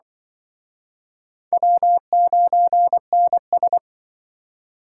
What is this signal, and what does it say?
Morse code, heard as a keyed tone, about 24 wpm, 700 Hz. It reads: W9NS
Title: Speed Racing - USA 1x2 Call Signs 20wpm